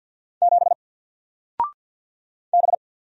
Morse code, heard as a keyed tone, about 48 wpm, 700 Hz. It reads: 7 B
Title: Single Letter-Number Alt 50wpm